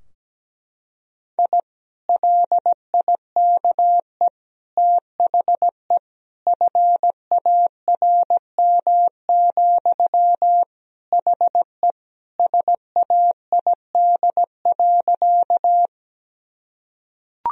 Morse code, heard as a keyed tone, about 17 wpm, 700 Hz. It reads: I LIKE THE FARM, HE SAID.
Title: Sentences from Top 1000 Words - Review Part 2 - 17wpm